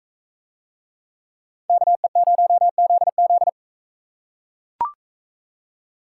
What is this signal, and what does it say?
Morse code, heard as a keyed tone, about 43 wpm, 700 Hz. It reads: KE0ZZ